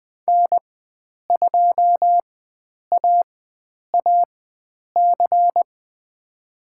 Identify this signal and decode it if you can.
Morse code, heard as a keyed tone, about 20 wpm, spaced as for 9 wpm, 700 Hz. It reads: N2AAC
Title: USA Call Signs - 5 Characters - 4x Character Spacing - 20wpm